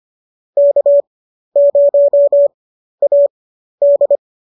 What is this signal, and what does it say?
Morse code, heard as a keyed tone, about 25 wpm, spaced as for 12 wpm, 570 Hz. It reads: K0AD
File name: qso008 12